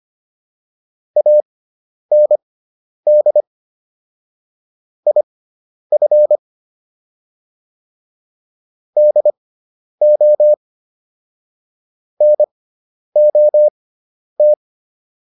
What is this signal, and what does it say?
Morse code, heard as a keyed tone, about 25 wpm, spaced as for 10 wpm, 600 Hz. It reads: AND IF DO NOT
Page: intermediate-practice-files